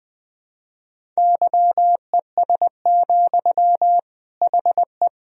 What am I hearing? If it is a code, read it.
Morse code, heard as a keyed tone, about 20 wpm, 700 Hz. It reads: YES, HE
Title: Sentences from Top 500 Words and Abbreviations 20wpm